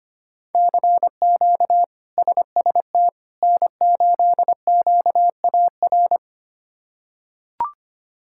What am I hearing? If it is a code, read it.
Morse code, heard as a keyed tone, about 25 wpm, 700 Hz. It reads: CQ SST N8QAR
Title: K1USN SST QSO Practice - 25wpm